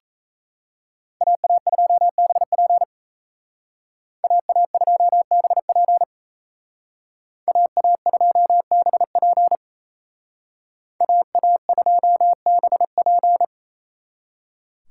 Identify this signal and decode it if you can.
Morse code, heard as a keyed tone, about 32 wpm, 700 Hz. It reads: AA2BP AA2BP AA2BP AA2BP
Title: Speed Racing - USA Call Signs 28wpm